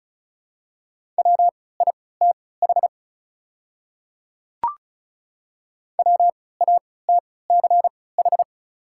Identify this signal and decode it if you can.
Morse code, heard as a keyed tone, about 34 wpm, spaced as for 20 wpm, 700 Hz. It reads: WITH WATCH
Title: 3x Character Spacing - Top 300 Words - 35wpm